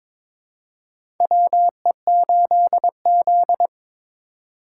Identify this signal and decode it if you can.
Morse code, heard as a keyed tone, about 21 wpm, 700 Hz. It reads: WE8Z
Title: USA Call Signs - 4 Characters - 3x Character Spacing - 22wpm